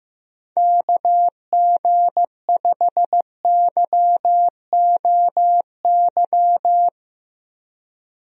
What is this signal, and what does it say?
Morse code, heard as a keyed tone, about 16 wpm, 700 Hz. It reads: KG5YOY
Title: USA Call Signs - 6 Characters - Repeated 3x - 15wpm